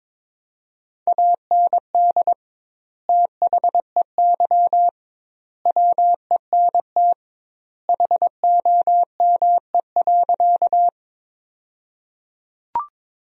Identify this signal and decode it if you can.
Morse code, heard as a keyed tone, about 22 wpm, 700 Hz. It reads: AND THEY WENT HOME.
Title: 2x Word-Spacing - Sentences from Top 200 Words - 22wpm